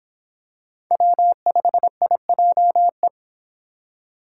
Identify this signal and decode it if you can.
Morse code, heard as a keyed tone, about 25 wpm, 700 Hz. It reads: W5IJE